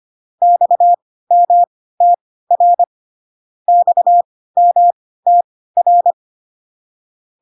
Morse code, heard as a keyed tone, about 25 wpm, spaced as for 16 wpm, 700 Hz. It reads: XMTR XMTR